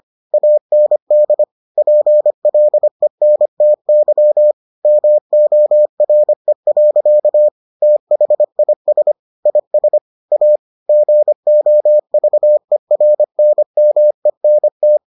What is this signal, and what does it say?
Morse code, heard as a keyed tone, about 25 wpm, 600 Hz. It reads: AND PLENTY MORE. THIS IS A GOVERNMENT